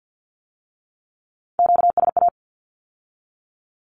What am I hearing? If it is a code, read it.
Morse code, heard as a keyed tone, about 50 wpm, 700 Hz. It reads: QSA
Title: Q-Codes with Spoken Meaning - 50wpm